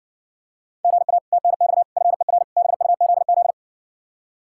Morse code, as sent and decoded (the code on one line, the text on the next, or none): --. .- / - -. -..- / ..-. .-. / -... ..- --.. --..
GA TNX FER BUZZ